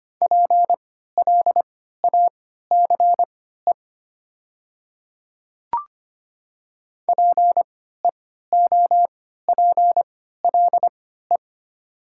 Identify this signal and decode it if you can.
Morse code, heard as a keyed tone, about 25 wpm, spaced as for 14 wpm, 700 Hz. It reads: PLACE PEOPLE
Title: 3x Character Spacing - Top 300 Words - 25wpm